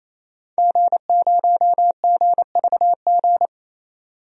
Morse code, heard as a keyed tone, about 28 wpm, 700 Hz. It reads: G0GVG
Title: DXCC-100 Call Signs - 5 Characters - 28wpm